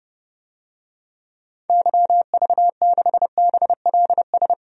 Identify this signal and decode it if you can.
Morse code, heard as a keyed tone, about 30 wpm, 700 Hz. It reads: YV6BLS
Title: DXCC-100 Call Signs - 6 Characters - 30wpm